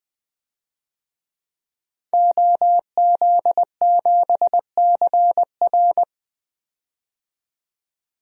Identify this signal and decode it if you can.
Morse code, heard as a keyed tone, about 20 wpm, 700 Hz. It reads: OZ7CR
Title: DXCC-100 Call Signs - 5 Characters - 20wpm